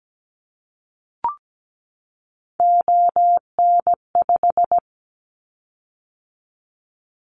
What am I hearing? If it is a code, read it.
Morse code, heard as a keyed tone, about 17 wpm, 700 Hz. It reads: ON5
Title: Callsign Prefixes for Top 25 DXCC - 17wpm